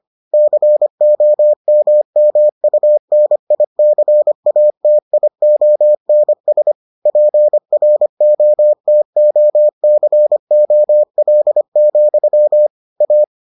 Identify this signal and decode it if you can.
Morse code, heard as a keyed tone, about 25 wpm, 600 Hz. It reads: COMMUNICATIONS PROTOCOL, A